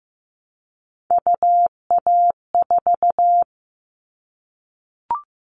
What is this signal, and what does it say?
Morse code, heard as a keyed tone, about 15 wpm, 700 Hz. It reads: UA4
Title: Callsign Prefixes for Top 25 DXCC - Repeated 3x - 15wpm